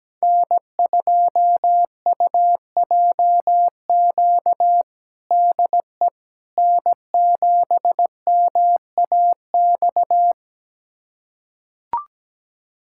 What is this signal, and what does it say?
Morse code, heard as a keyed tone, about 17 wpm, 700 Hz. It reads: N2UJQ DE N7MAX
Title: USA Callsign DE Callsign - 5 Characters - 17wpm